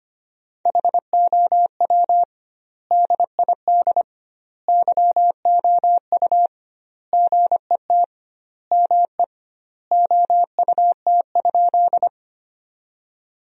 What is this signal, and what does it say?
Morse code, heard as a keyed tone, about 25 wpm, 700 Hz. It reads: HOW DID YOU GET ME OUT?
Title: 2x Word-Spacing - Sentences from Top 200 Words - 25wpm